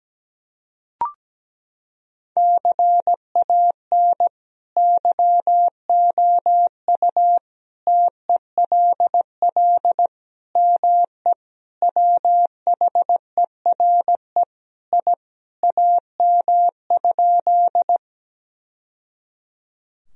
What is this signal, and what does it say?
Morse code, heard as a keyed tone, about 17 wpm, 700 Hz. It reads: CAN YOU TELL ME WHERE I AM?
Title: Sentences from Top 1000 Words - Review Part 2 - 17wpm